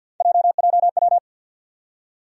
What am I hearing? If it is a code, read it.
Morse code, heard as a keyed tone, about 50 wpm, 700 Hz. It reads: JJW
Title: USA Call Sign Postfixes - 3 Letters 50wpm